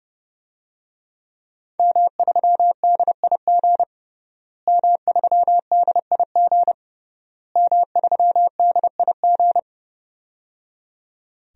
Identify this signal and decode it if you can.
Morse code, heard as a keyed tone, about 30 wpm, 700 Hz. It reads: M3DIG M3DIG M3DIG
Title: DXCC-100 Call Signs - 5 Characters - Repeated 3x - 30wpm